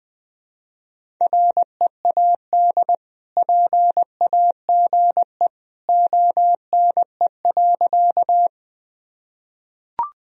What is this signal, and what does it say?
Morse code, heard as a keyed tone, about 20 wpm, 700 Hz. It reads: READ PAGE ONE.
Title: Sentences from Top 1000 Words - Review Part 1 - 20wpm